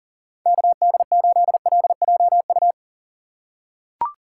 Morse code, heard as a keyed tone, about 39 wpm, 700 Hz. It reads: KD8LJU
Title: USA Call Signs - 6 Characters - 4x Character Spacing - 40wpm